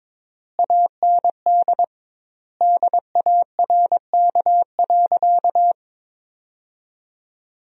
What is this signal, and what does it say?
Morse code, heard as a keyed tone, about 22 wpm, 700 Hz. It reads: AND DARK.
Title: 2x Word-Spacing - Sentences from Top 500 Words - 22wpm